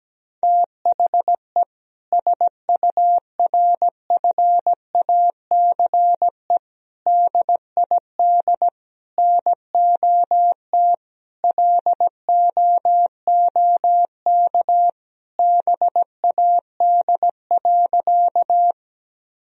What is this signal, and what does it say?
Morse code, heard as a keyed tone, about 17 wpm, 700 Hz. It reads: THE SURFACE DID NOT LOOK BAD.
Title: Sentences from Top 1000 Words - Review Part 3 - 17wpm